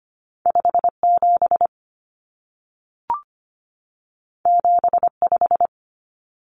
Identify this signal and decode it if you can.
Morse code, heard as a keyed tone, about 25 wpm, 700 Hz. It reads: 57 75
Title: Common Cut Numbers - 2-Digit 25wpm